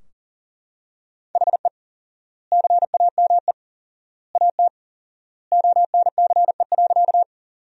Morse code, heard as a keyed tone, about 37 wpm, 700 Hz. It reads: HE CAME AT ONCE.
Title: 4x Word-Spacing - Sentences from Top 300 Words - 40wpm